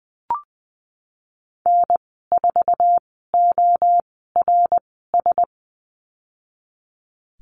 Morse code, heard as a keyed tone, about 20 wpm, spaced as for 15 wpm, 700 Hz. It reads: N4ORS
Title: 082 Lesson 14g - Number 2 - Call Signs 20wpm